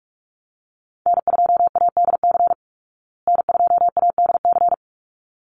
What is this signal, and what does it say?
Morse code, heard as a keyed tone, about 44 wpm, 700 Hz. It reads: N2ADC N2ADC
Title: Call Signs Repeated 3x - 45wpm